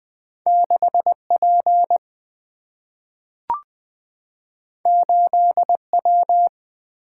Morse code, heard as a keyed tone, about 20 wpm, 690 Hz. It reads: 6P 8W
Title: Two Letter-Number Combinations - 20wpm